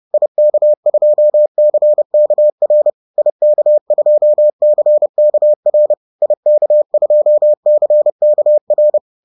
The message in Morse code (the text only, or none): IK2CKR IK2CKR IK2CKR